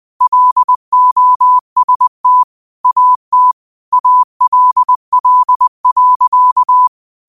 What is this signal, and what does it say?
Morse code, heard as a keyed tone, about 20 wpm, 1000 Hz. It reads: LOST AT ALL.
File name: qotdfiles / 20wpm